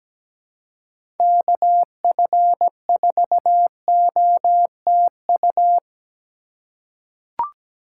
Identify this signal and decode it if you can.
Morse code, heard as a keyed tone, about 17 wpm, 700 Hz. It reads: KF4OTU
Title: USA Call Signs - 6 Characters - 2x Character Spacing - 17wpm